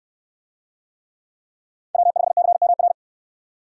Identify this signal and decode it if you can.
Morse code, heard as a keyed tone, about 68 wpm, 700 Hz. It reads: W5CMR